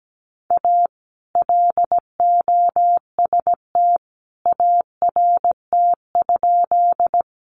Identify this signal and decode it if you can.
Morse code, heard as a keyed tone, about 17 wpm, 700 Hz. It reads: A LOST ART?
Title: Sentences from Top 1400 Words 17wpm